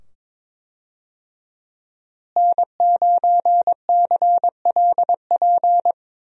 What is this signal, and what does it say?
Morse code, heard as a keyed tone, about 21 wpm, 700 Hz. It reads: N9CLP
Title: USA Call Signs - 5 Characters - Repeated 2x - 22wpm